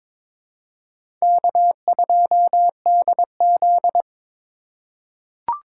Morse code, heard as a keyed tone, about 22 wpm, 700 Hz. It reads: K2DZ
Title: USA Call Signs - 4x Character Spacing - 22wpm